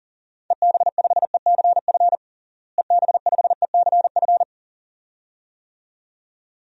Morse code, heard as a keyed tone, about 37 wpm, 700 Hz. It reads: EB5ECF EB5ECF
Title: DXCC-100 Call Signs - 6 Characters - Repeated 2x - 40wpm